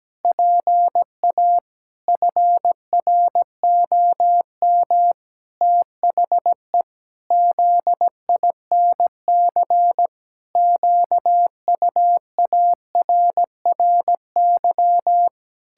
Morse code, heard as a keyed tone, about 17 wpm, 700 Hz. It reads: PA FROM THE ZINC QUARRY
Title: Pangram Challenge v2 - 17wpm